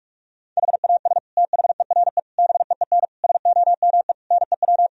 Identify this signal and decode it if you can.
Morse code, heard as a keyed tone, about 41 wpm, 700 Hz. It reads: HAS THERE BEEN SOME NEW
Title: Sentences from Top 1000 Words - Review Part 4 - 45wpm